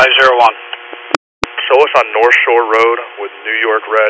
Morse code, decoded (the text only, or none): none